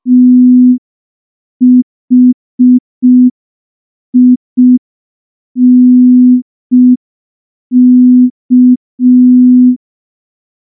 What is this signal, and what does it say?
Morse code, heard as a keyed tone, about 5 wpm, 250 Hz. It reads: THINK
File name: morse / morse-training-05WPM